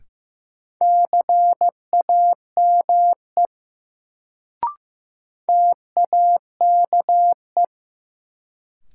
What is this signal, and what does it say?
Morse code, heard as a keyed tone, about 15 wpm, 700 Hz. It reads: CAME TAKE
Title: Sentences from Top 200 Words - New Word Drill - 15wpm